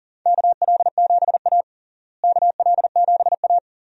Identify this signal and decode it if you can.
Morse code, heard as a keyed tone, about 40 wpm, 700 Hz. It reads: KL7A KL7A
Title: DXCC-100 Call Signs - 4 Characters - Repeated 3x - 40wpm